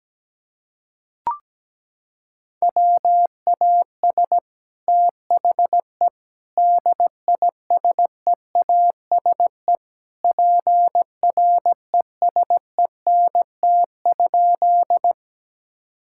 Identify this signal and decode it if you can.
Morse code, heard as a keyed tone, about 17 wpm, 700 Hz. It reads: WAS THE DISEASE PRESENT?
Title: Sentences from Top 1000 Words - Review Part 6 - 17wpm